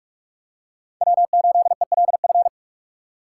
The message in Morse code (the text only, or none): W8ELF